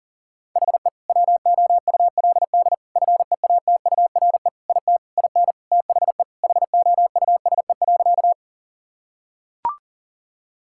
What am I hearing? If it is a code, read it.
Morse code, heard as a keyed tone, about 40 wpm, 700 Hz. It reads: HE WOULD FEATURE IT IN THE HOUSE.
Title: Sentences from Top 1000 Words - Review Part 5 - 40wpm